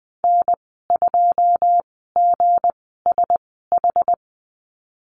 Morse code, heard as a keyed tone, about 20 wpm, spaced as for 14 wpm, 700 Hz. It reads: N2GSH
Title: 167 Lesson 27g - Number 7 - Call Signs 20wpm